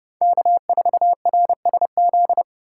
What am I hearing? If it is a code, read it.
Morse code, heard as a keyed tone, about 29 wpm, 700 Hz. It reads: K4RSZ